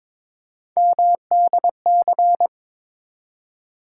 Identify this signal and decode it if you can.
Morse code, heard as a keyed tone, about 22 wpm, 700 Hz. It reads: MDC
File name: ARRL Sections v2 - 22wpm